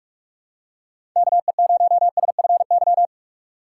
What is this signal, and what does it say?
Morse code, heard as a keyed tone, about 44 wpm, 700 Hz. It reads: KE0SFY
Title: USA Call Signs - 6 Characters - 3x Character Spacing - 45wpm